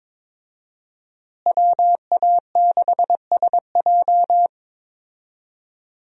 Morse code, heard as a keyed tone, about 22 wpm, 700 Hz. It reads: WA6SJ
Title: USA Call Signs - 5 Characters - Repeated 2x - 22wpm